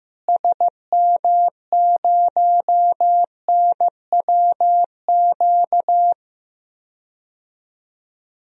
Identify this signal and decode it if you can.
Morse code, heard as a keyed tone, about 15 wpm, 700 Hz. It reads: SM0NWQ
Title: DXCC-100 Call Signs - 6 Characters - Repeated 3x - 15wpm